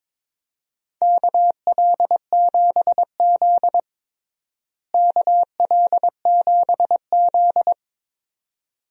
Morse code, heard as a keyed tone, about 22 wpm, 700 Hz. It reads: KL7Z KL7Z
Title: DXCC-100 Call Signs - 4 Characters - Repeated 2x - 22wpm